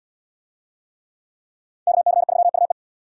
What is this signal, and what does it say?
Morse code, heard as a keyed tone, about 68 wpm, 700 Hz. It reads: KC3UE